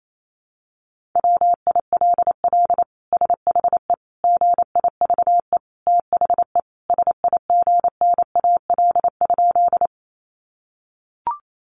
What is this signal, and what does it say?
Morse code, heard as a keyed tone, about 28 wpm, 700 Hz. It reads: WILL SHE GIVE THE SIGNAL?
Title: Sentences from Top 1400 Words 28wpm